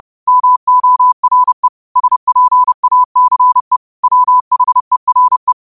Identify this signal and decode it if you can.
Morse code, heard as a keyed tone, about 30 wpm, 1000 Hz. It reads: MORE SPACE WHERE